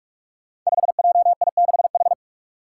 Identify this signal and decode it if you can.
Morse code, heard as a keyed tone, about 46 wpm, 700 Hz. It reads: 5JI6H
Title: Five Challenging Letter-Number Combinations 45wpm